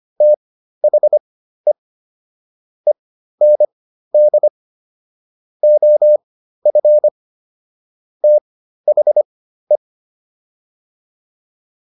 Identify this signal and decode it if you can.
Morse code, heard as a keyed tone, about 25 wpm, spaced as for 13 wpm, 600 Hz. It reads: THE END OF THE